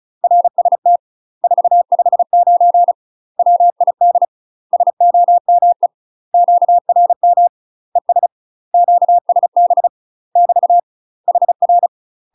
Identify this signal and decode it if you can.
Morse code, heard as a keyed tone, about 35 wpm, 700 Hz. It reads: RST 459 WID SOME QRM ES QSB <BT> HR
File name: UK MASTER QSOs TRAINING / 35WPM / UK MASTER QSOs 0041